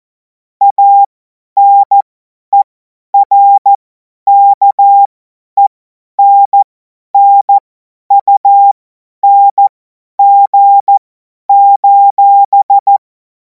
Morse code, heard as a keyed tone, about 14 wpm, spaced as for 10 wpm, 800 Hz. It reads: ANERKENNUNG:
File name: Morsekurs / Witze / Witz 28. April 2026 50 F70